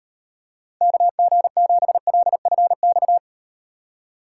Morse code, heard as a keyed tone, about 38 wpm, 700 Hz. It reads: KG7LFX